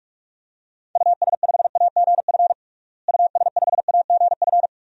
Morse code, heard as a keyed tone, about 46 wpm, 700 Hz. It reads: US5AGF US5AGF
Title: DXCC-100 Call Signs - 6 Characters - Repeated 2x - 45wpm